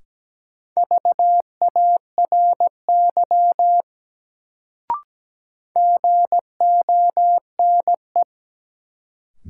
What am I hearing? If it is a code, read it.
Morse code, heard as a keyed tone, about 17 wpm, 700 Hz. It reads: VARY GONE
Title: Sentences from Top 1200 Words - New Word Drill - 17wpm